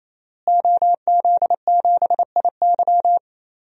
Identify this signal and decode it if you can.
Morse code, heard as a keyed tone, about 28 wpm, 700 Hz. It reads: OZ7IY
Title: DXCC-100 Call Signs - 5 Characters - Repeated 3x - 28wpm